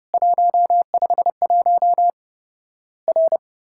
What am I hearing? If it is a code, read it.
Morse code, heard as a keyed tone, about 30 wpm, 700 Hz. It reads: 151 R
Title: Worked All Europe DX Contest - QTC Exchange - 30wpm